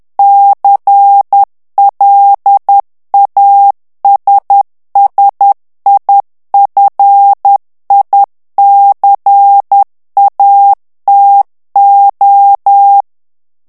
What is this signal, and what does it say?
Morse code, heard as a keyed tone, about 10 wpm, 800 Hz. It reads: CLASSIFICATO
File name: ITALIEN texte clair 2 vitesse 10 mots minute